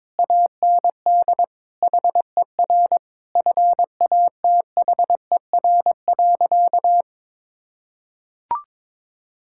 Morse code, heard as a keyed tone, about 22 wpm, 700 Hz. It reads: AND HER FATHER.
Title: Sentences from Top 1000 Words - Review Part 6 - 22wpm